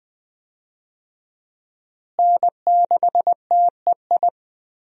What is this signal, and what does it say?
Morse code, heard as a keyed tone, about 20 wpm, 700 Hz. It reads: N6TEI
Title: USA Call Signs - 5 Characters - Repeated 2x - 20wpm